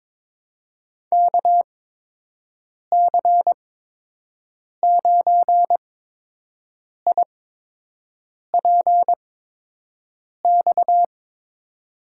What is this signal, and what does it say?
Morse code, heard as a keyed tone, about 22 wpm, spaced as for 6 wpm, 700 Hz. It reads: KC9IPX
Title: USA Call Signs - 6 Characters - 8x Character Spacing - 22wpm